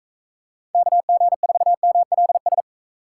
Morse code, heard as a keyed tone, about 42 wpm, 700 Hz. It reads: KG4MLS